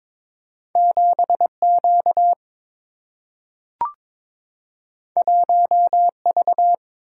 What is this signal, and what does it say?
Morse code, heard as a keyed tone, about 22 wpm, 700 Hz. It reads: 7Q 1V
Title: Single-Letter Single-Number 22wpm